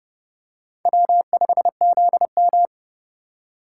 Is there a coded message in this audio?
Yes: Morse code from 0.9 to 2.7 s, heard as a keyed tone, about 29 wpm, 700 Hz. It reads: W5ZM